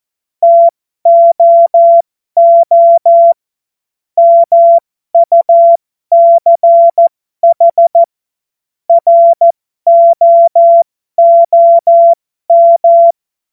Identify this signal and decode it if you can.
Morse code, heard as a keyed tone, about 14 wpm, spaced as for 12 wpm, 680 Hz. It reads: TOO MUCH ROOM
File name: Morsekurs / Witze